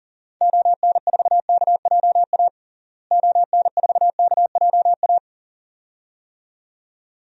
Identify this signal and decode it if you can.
Morse code, heard as a keyed tone, about 37 wpm, 700 Hz. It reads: ON4KJA ON4KJA
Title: DXCC-100 Call Signs - 6 Characters - Repeated 2x - 40wpm